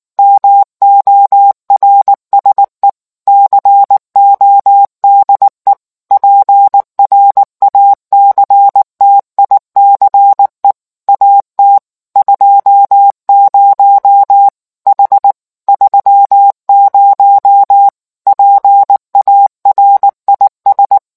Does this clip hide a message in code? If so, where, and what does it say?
Yes: Morse code from 0.2 to 21.0 s, heard as a keyed tone, about 19 wpm, 800 Hz. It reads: MORSE CODE PRACTICE AT 20 H 30 PARIS